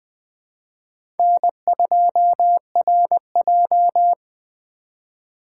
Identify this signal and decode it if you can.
Morse code, heard as a keyed tone, about 20 wpm, 700 Hz. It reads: N2RJ